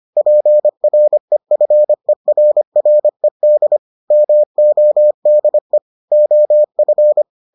Morse code, heard as a keyed tone, about 25 wpm, 600 Hz. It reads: PREFERRED MODE OF